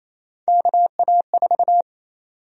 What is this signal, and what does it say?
Morse code, heard as a keyed tone, about 28 wpm, 700 Hz. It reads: KA4
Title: USA Call Sign Prefixes - Repeated 2x - 28wpm